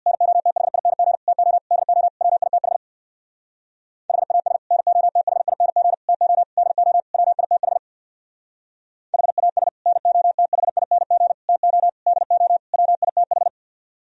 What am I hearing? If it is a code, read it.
Morse code, heard as a keyed tone, about 61 wpm, 700 Hz. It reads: NOTHING TO DO WITH HAS NOTHING TO DO WITH HAS NOTHING TO DO WITH
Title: Speed-Racing - Sets of 5 Words 50wpm